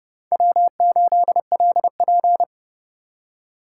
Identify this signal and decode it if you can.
Morse code, heard as a keyed tone, about 29 wpm, 700 Hz. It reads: W8LP